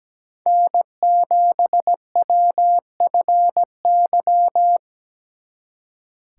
Morse code, heard as a keyed tone, about 20 wpm, 700 Hz. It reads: N7WFY
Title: Speed Racing - USA 1x3 Call Signs 17wpm